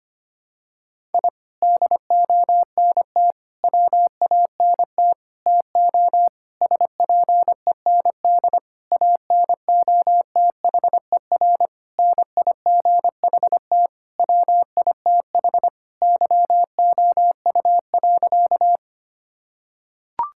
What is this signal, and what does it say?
Morse code, heard as a keyed tone, about 25 wpm, 700 Hz. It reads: I DONT WANT TO SPEND ANOTHER NIGHT WITH YOU.
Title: Sentences from Top 1000 Words - Review Part 5 - 25wpm